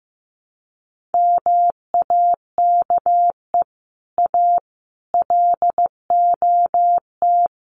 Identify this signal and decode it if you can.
Morse code, heard as a keyed tone, about 15 wpm, 700 Hz. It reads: MAKE A LOT
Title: Sets of 5 Words Encore2 15wpm